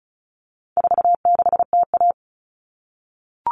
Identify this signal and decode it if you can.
Morse code, heard as a keyed tone, about 34 wpm, 700 Hz. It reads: 46TA
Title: All Cut Numbers - 4-Digit 35wpm